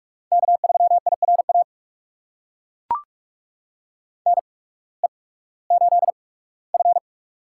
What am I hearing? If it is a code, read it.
Morse code, heard as a keyed tone, about 42 wpm, spaced as for 21 wpm, 700 Hz. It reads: K3IRA NE8F
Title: USA Call Signs - 8x Character Spacing - 45wpm